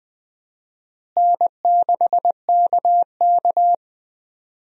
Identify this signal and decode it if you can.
Morse code, heard as a keyed tone, about 20 wpm, 700 Hz. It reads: N6KK
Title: USA Call Signs - 4 Characters - Repeated 2x - 20wpm